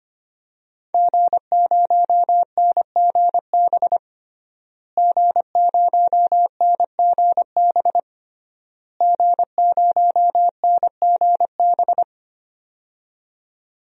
Morse code, heard as a keyed tone, about 25 wpm, 700 Hz. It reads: G0NGB G0NGB G0NGB
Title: DXCC-100 Call Signs - 5 Characters - Repeated 3x - 25wpm